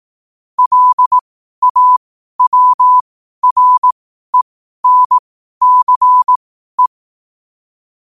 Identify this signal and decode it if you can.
Morse code, heard as a keyed tone, about 18 wpm, spaced as for 12 wpm, 1000 Hz. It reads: LAWRENCE
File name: qotdfiles / 12wpm